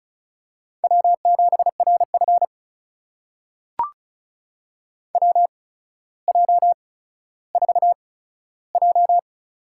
Morse code, heard as a keyed tone, about 36 wpm, spaced as for 15 wpm, 700 Hz. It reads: W7RF WJ4J
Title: USA Call Signs - 4 Characters - 8x Character Spacing - 35wpm